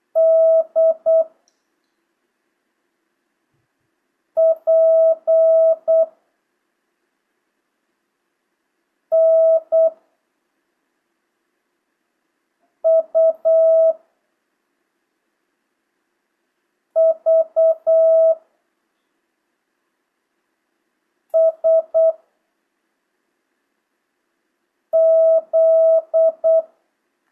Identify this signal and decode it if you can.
Morse code, heard as a keyed tone, about 8 wpm, 650 Hz. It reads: D P N U V S Z
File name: jh8cbh / 2025cw